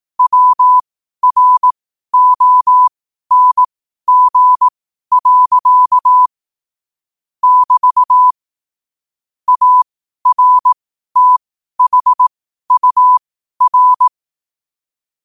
Morse code, heard as a keyed tone, about 18 wpm, spaced as for 12 wpm, 1000 Hz. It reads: WRONG. <BT> ARTHUR
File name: qotdfiles / 12wpm